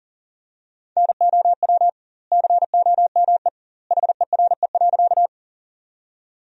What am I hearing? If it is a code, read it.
Morse code, heard as a keyed tone, about 40 wpm, 700 Hz. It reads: NOW COME HERE.
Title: 2x Word-Spacing - Sentences from Top 200 Words - 40wpm